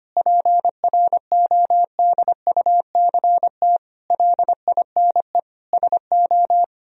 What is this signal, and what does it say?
Morse code, heard as a keyed tone, about 25 wpm, 700 Hz. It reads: PRODUCT LINE SO
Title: Sentences from Top 1000 Words - Review Part 3 - 25wpm